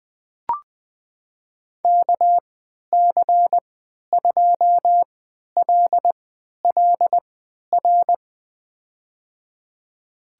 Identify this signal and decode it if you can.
Morse code, heard as a keyed tone, about 20 wpm, spaced as for 11 wpm, 700 Hz. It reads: KC2LLR
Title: USA Call Signs - 6 Characters - 3x Character Spacing - 20wpm